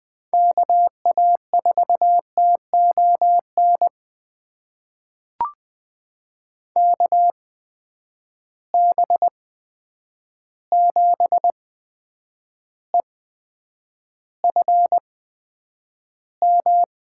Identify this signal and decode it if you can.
Morse code, heard as a keyed tone, about 20 wpm, spaced as for 9 wpm, 700 Hz. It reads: KA4TON KB7EFM
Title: USA Call Signs - 6 Characters - 8x Character Spacing - 20wpm